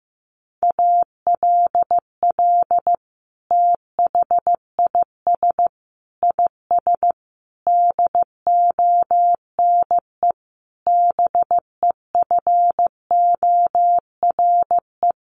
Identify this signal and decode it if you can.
Morse code, heard as a keyed tone, about 14 wpm, 700 Hz. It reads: ALL THIS IS DONE BEFORE
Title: Sentences from Top 1500 Words 15wpm